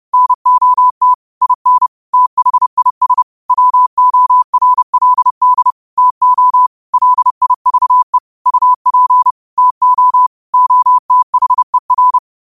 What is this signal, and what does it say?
Morse code, heard as a keyed tone, about 30 wpm, 1000 Hz. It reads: NOT IN THIS WORLD TO LIVE UP TO OTHER
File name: qotdfiles / 30wpm